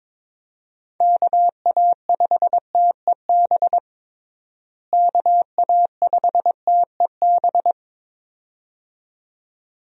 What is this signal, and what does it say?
Morse code, heard as a keyed tone, about 22 wpm, 700 Hz. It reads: KA5TEB KA5TEB
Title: USA Call Signs - 6 Characters - Repeated 2x - 22wpm